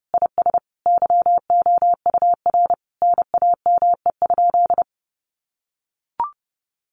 Morse code, heard as a keyed tone, about 30 wpm, 700 Hz. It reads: IS YOUR NAME?
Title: Sentences from Top 200 Words 30wpm